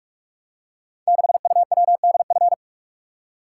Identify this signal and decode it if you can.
Morse code, heard as a keyed tone, about 46 wpm, 700 Hz. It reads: 6UWDF
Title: Code Groups - 5 Letter:Numbers - 45wpm